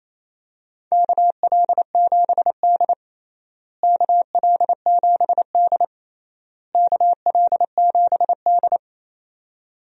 Morse code, heard as a keyed tone, about 28 wpm, 700 Hz. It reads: KL7D KL7D KL7D
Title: DXCC-100 Call Signs - 4 Characters - Repeated 3x - 28wpm